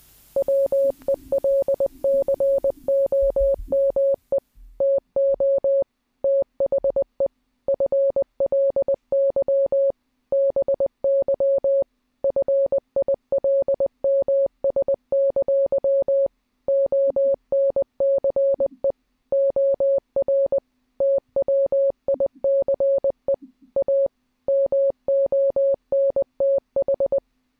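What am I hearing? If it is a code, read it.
Morse code, heard as a keyed tone, about 20 wpm, 550 Hz. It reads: WELCOME TO THE FLY BY FILMS! ONCE OR TWICE A MONTH
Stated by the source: Morse clip taken from the film Catch Me If You Can.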